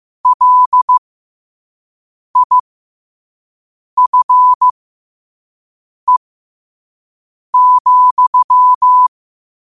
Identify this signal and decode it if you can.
Morse code, heard as a keyed tone, about 15 wpm, spaced as for 5 wpm, 1000 Hz. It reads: LIFE,
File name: 5wpm